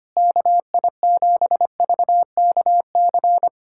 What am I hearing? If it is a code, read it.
Morse code, heard as a keyed tone, about 25 wpm, 700 Hz. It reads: KI7VKC
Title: USA Callsign DE Callsign - 6 Characters - 25wpm